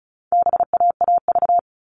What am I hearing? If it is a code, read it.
Morse code, heard as a keyed tone, about 34 wpm, 700 Hz. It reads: BAAV
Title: All Cut Numbers - 4-Digit 35wpm